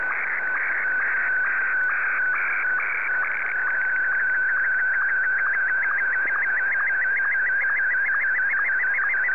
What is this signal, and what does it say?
SSTV (slow-scan television), Martin 1.